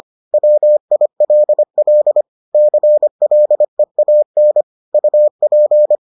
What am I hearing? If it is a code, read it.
Morse code, heard as a keyed tone, about 25 wpm, 600 Hz. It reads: WILL CLEAN UP